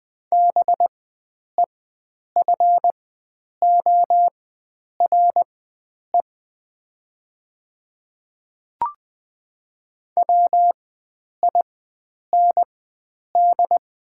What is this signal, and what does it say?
Morse code, heard as a keyed tone, about 20 wpm, spaced as for 9 wpm, 700 Hz. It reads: BEFORE WIND
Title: 4x Character Spacing - Top 500 Words - 20wpm